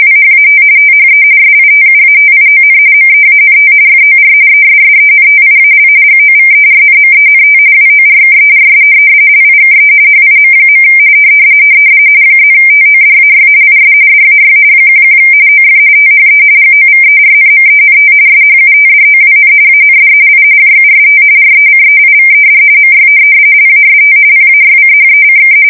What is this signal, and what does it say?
RTTY, about 45.45 baud. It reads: WHAT THE SITUATION MEANT FOR ESSENTIAL GOVERNMENT WORKERS LIKE HIM. +TODAY IS DAY 11 OF THE GOVERNMENT SHUTDOWN AND I JUST RECEIVED MY LAST PAYCHECK
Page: 170 Shift 2125/2295